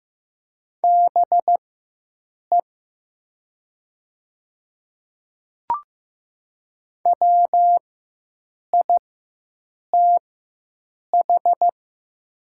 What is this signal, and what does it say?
Morse code, heard as a keyed tone, about 15 wpm, spaced as for 7 wpm, 700 Hz. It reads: BE WITH
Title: 4x Character Spacing - Top 200 Words - 15wpm